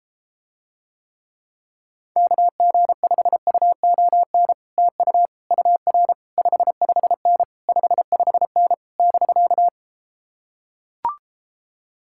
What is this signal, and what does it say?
Morse code, heard as a keyed tone, about 33 wpm, 700 Hz. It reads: KG5UON TU UR 55N 55N <BK>